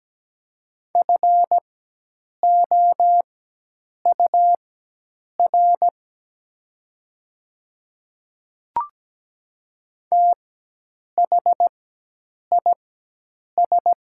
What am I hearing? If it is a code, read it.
Morse code, heard as a keyed tone, about 17 wpm, spaced as for 8 wpm, 700 Hz. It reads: FOUR THIS
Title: 4x Character Spacing - Top 300 Words - 17wpm